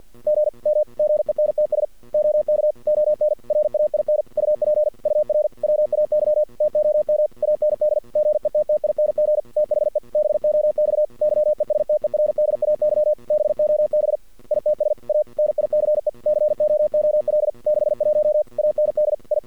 Morse code, heard as a keyed tone, about 50 wpm, 610 Hz. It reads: G M GENTS OK ON RAIN UP UR WAY TOM AND GETTING THE BOX YESTERDAY BOB ITS A NICE COOL 61 AND I
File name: archived / CRSnet